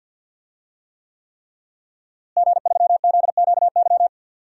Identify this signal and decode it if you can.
Morse code, heard as a keyed tone, about 51 wpm, 700 Hz. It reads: G3ZQY
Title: DXCC-100 Call Signs - 5 Characters - Repeated 3x - 50wpm